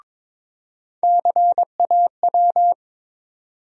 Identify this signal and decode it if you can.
Morse code, heard as a keyed tone, about 22 wpm, 700 Hz. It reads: CAW
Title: USA Call Sign Postfixes - 3 Letters 22wpm